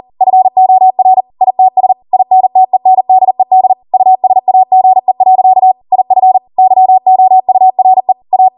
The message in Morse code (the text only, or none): FOR ITS INTENDED USAGE. IF YOURE A